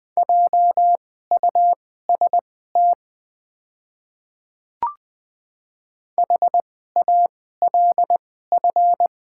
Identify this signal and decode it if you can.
Morse code, heard as a keyed tone, about 20 wpm, spaced as for 15 wpm, 690 Hz. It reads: JUST HALF
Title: 2x Character Spacing - Top 500 Words - 20wpm